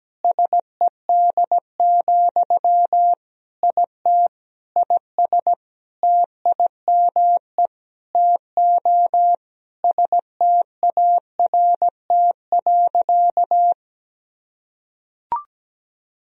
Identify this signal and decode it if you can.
Morse code, heard as a keyed tone, about 17 wpm, 700 Hz. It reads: SED, IT IS TIME TO START.
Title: Sentences from Top 300 Words and Abbreviations 17wpm